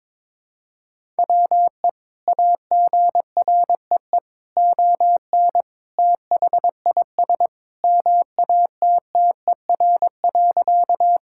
Morse code, heard as a keyed tone, about 22 wpm, 700 Hz. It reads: WE AGREE ON THIS MATTER.
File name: practice / 22wpm / Sentences with 2-Word Phrases - 22wpm